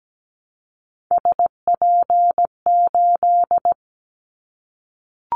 Morse code, heard as a keyed tone, about 17 wpm, 700 Hz. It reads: SP8
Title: Callsign Prefixes for Top 25 DXCC - Repeated 3x - 17wpm